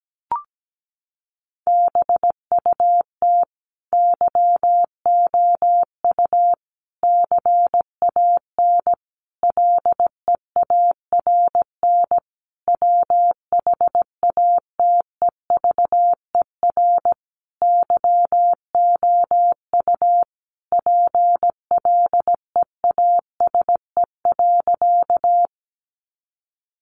Morse code, heard as a keyed tone, about 17 wpm, 700 Hz. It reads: BUT YOU CAN LEARN WHATEVER YOU PLEASE.
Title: Sentences from Top 1500 Words 17wpm